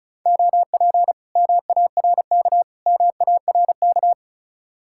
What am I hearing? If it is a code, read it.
Morse code, heard as a keyed tone, about 45 wpm, 700 Hz. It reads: OP MARK MARK